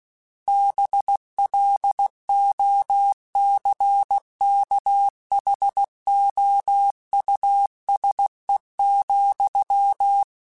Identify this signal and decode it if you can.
Morse code, heard as a keyed tone, about 16 wpm, 800 Hz. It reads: BLOCKHOUSE,
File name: General / General-mp3s / Houses @15